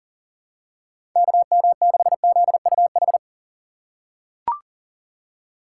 Morse code, heard as a keyed tone, about 40 wpm, 700 Hz. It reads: KM6ZUH
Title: USA Call Signs - 6 Characters - 3x Character Spacing - 40wpm